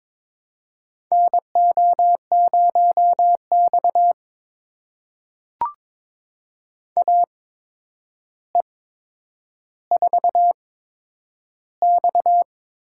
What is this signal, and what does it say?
Morse code, heard as a keyed tone, about 22 wpm, spaced as for 9 wpm, 700 Hz. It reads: NO0X AE4X
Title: USA Call Signs - 4 Characters - 8x Character Spacing - 22wpm